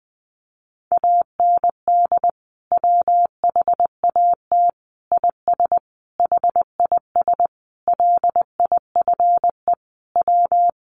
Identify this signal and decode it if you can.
Morse code, heard as a keyed tone, about 20 wpm, 700 Hz. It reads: AND WHAT IS HIS LIFE W
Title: Sentences from Top 800 Words 20wpm